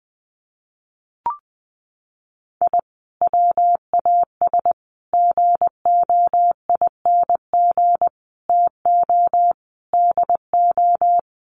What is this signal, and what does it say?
Morse code, heard as a keyed tone, about 20 wpm, 700 Hz. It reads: I WAS GOING TO DO
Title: Sets of 5 Words Encore2 20wpm